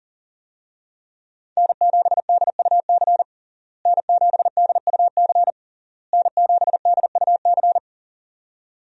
Morse code, heard as a keyed tone, about 40 wpm, 700 Hz. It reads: N7DUC N7DUC N7DUC
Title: USA Call Signs - 5 Characters - Repeated 3x - 40wpm